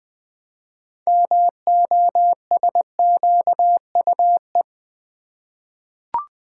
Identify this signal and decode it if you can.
Morse code, heard as a keyed tone, about 20 wpm, 700 Hz. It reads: MOSQUE
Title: 6 Letter Words - First 3 Letters Given - 20wpm